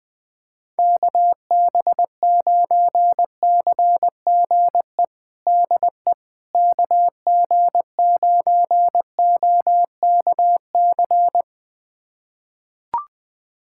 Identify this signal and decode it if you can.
Morse code, heard as a keyed tone, about 20 wpm, 700 Hz. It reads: KB9CGE DE KG9OKC
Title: USA Callsign DE Callsign - 6 Characters - 20wpm